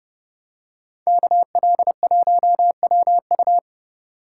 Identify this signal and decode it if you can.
Morse code, heard as a keyed tone, about 29 wpm, 700 Hz. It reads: KL1WU